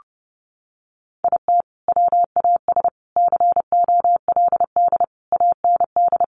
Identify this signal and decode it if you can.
Morse code, heard as a keyed tone, about 28 wpm, 700 Hz. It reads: IT WAS COLD AND
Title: Sentences from Top 500 Words 30wpm